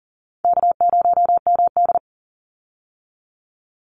Morse code, heard as a keyed tone, about 40 wpm, 700 Hz. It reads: K0MD
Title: Easy Call Signs - 40wpm